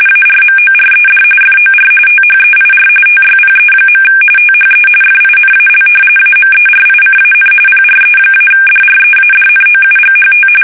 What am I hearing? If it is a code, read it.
RTTY, about 45.45 baud. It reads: ARMORED FOR THEIR TYPE. THE FINAL AMERICAN PRE-DREADNOUGHT
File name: fsk850 / battleships